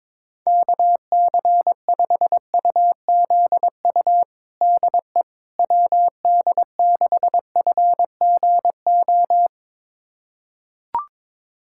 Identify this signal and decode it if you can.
Morse code, heard as a keyed tone, about 22 wpm, 700 Hz. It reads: KC5UZU DE WD6FGO
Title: USA Callsign DE Callsign 22wpm